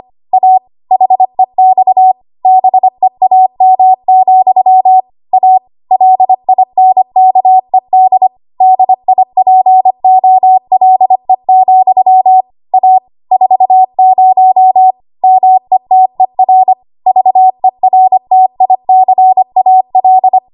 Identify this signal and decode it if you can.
Morse code, heard as a keyed tone, about 25 wpm, 750 Hz. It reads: A HEX BEAM, A LINKED DIPOLE, A 40 METER VERTICAL
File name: Archive / 25 WPM / 25WPM